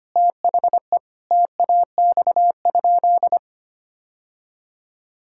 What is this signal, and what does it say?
Morse code, heard as a keyed tone, about 25 wpm, 700 Hz. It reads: THE TAX?
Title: Sentences from Top 1000 Words - Review Part 5 - 25wpm